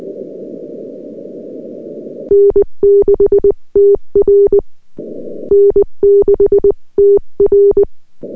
Morse code, heard as a keyed tone, about 20 wpm, 400 Hz. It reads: N6TR N6TR
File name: n6tr / VP8-LZ1UQ-N6TR